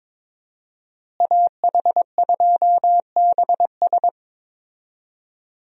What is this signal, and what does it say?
Morse code, heard as a keyed tone, about 22 wpm, 700 Hz. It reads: AH2BS
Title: USA Call Signs - 5 Characters - Repeated 3x - 22wpm